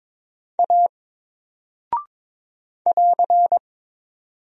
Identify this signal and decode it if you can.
Morse code, heard as a keyed tone, about 22 wpm, 700 Hz. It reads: A <AR>
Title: Single Character (Letter:Number:Punctuation:Prosign) Alt 22wpm